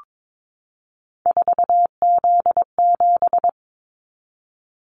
Morse code, heard as a keyed tone, about 22 wpm, 700 Hz. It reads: 4Z7
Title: Callsign Prefixes for Top 50 DXCC - 22wpm